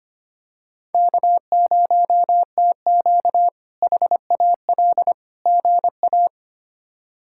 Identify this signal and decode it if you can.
Morse code, heard as a keyed tone, about 25 wpm, 700 Hz. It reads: K0TQ HAL GA
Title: K1USN SST QSO Practice - 25wpm